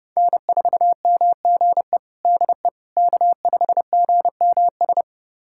Morse code, heard as a keyed tone, about 32 wpm, 700 Hz. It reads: N4MGE DE K5GMS
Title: USA Callsign DE Callsign - 5 Characters - 30wpm